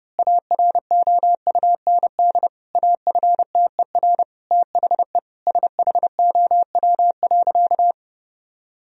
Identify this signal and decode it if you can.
Morse code, heard as a keyed tone, about 30 wpm, 700 Hz. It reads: AROUND AFTER THE SHOW.
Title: Sentences from Top 1000 Words - Review Part 3 - 30wpm